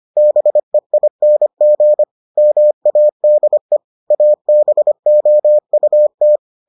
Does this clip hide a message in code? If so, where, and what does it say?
Yes: Morse code from 0.2 to 6.4 s, heard as a keyed tone, about 25 wpm, 600 Hz. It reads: BEING MADE ABOUT